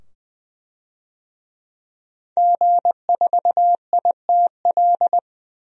Morse code, heard as a keyed tone, about 20 wpm, 700 Hz. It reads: G4ITL
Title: DXCC-100 Call Signs - 5 Characters - 20wpm